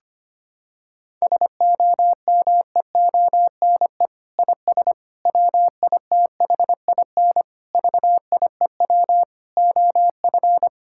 Morse code, heard as a keyed tone, about 25 wpm, 700 Hz. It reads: SOMEONE IS WITHIN VIEW OF
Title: Sentences from Top 1000 Words - Review Part 6 - 25wpm